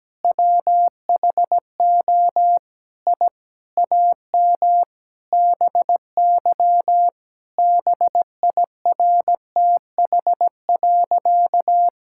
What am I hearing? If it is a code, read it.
Morse code, heard as a keyed tone, about 17 wpm, 700 Hz. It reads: WHO I AM BY BIRTH.
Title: Sentences from Top 1000 Words - Review Part 5 - 17wpm